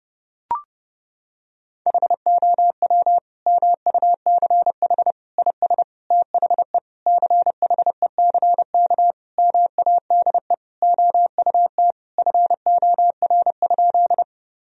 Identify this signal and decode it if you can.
Morse code, heard as a keyed tone, about 30 wpm, 700 Hz. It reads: HOW MUCH IS THE CHECK MADE OUT FOR?
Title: Sentences from Top 1000 Words - Review Part 3 - 30wpm